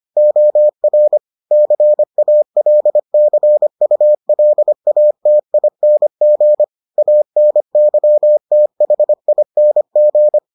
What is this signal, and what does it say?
Morse code, heard as a keyed tone, about 25 wpm, 600 Hz. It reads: OR CALCULATING ANYTHING